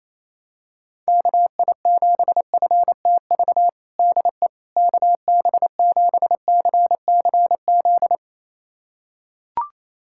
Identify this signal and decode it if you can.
Morse code, heard as a keyed tone, about 28 wpm, 700 Hz. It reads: KI7FTV DE KB7CCZ
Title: USA Callsign DE Callsign - 6 Characters - 28wpm